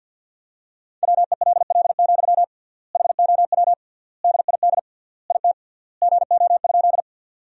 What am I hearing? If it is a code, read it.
Morse code, heard as a keyed tone, about 50 wpm, 700 Hz. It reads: WELL, HOW DID IT GO?
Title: 3x Word-Spacing - Sentences from Top 200 Words - 50wpm